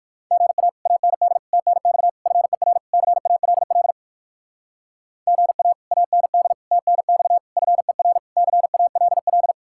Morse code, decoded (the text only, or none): GA AND TNX FER CALL GA AND TNX FER CALL